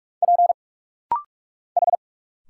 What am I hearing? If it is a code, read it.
Morse code, heard as a keyed tone, about 46 wpm, 700 Hz. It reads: P H
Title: Single Letter-Number Alt - Rapid-Fire 45wpm